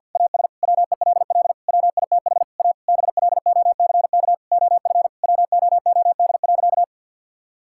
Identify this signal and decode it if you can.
Morse code, heard as a keyed tone, about 49 wpm, 700 Hz. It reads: AS WELL WITH A BLOCK OF WOOD.
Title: Sentences from Top 1600 Words 50wpm